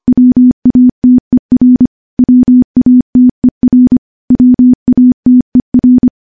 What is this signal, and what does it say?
Morse code, heard as a keyed tone, about 25 wpm, 250 Hz. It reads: WATER WATER WATER
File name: morse / morse-training-25WPM